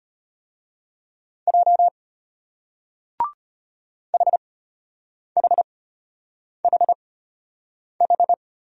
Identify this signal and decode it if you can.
Morse code, heard as a keyed tone, about 33 wpm, spaced as for 8 wpm, 700 Hz. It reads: J HHHH